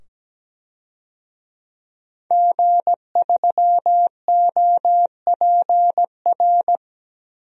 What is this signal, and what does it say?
Morse code, heard as a keyed tone, about 17 wpm, 700 Hz. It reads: G3OPR
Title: DXCC-100 Call Signs - 5 Characters - 17wpm